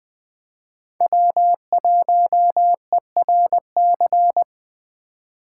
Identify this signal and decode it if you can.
Morse code, heard as a keyed tone, about 20 wpm, 700 Hz. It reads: W1ERC
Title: Speed Racing - USA 1x3 Call Signs 17wpm